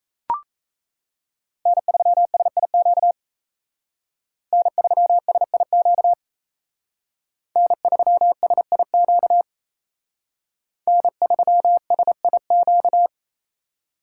Morse code, heard as a keyed tone, about 35 wpm, 700 Hz. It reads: N3SIQ N3SIQ N3SIQ N3SIQ
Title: Speed Racing - USA 1x3 Call Signs 28wpm